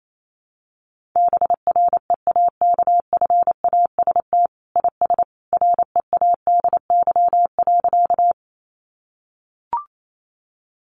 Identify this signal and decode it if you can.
Morse code, heard as a keyed tone, about 28 wpm, 700 Hz. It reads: BREAKFAST IS READY.
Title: Sentences from Top 1500 Words 28wpm